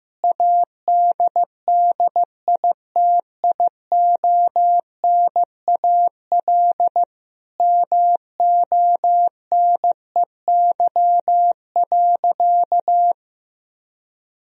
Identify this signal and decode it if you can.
Morse code, heard as a keyed tone, about 15 wpm, 700 Hz. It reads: ADDITIONAL MONEY.
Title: Sentences from Top 1000 Words - Review Part 4 - 15wpm